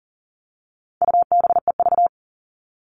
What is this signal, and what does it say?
Morse code, heard as a keyed tone, about 40 wpm, 700 Hz. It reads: UBEV
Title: All Cut Numbers - 4-Digit 40wpm